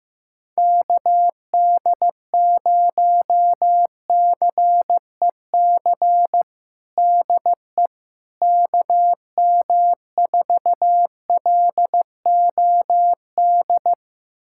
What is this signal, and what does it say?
Morse code, heard as a keyed tone, about 15 wpm, 700 Hz. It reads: KD0CEC DE KM4LOD
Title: USA Callsign DE Callsign - 6 Characters - 15wpm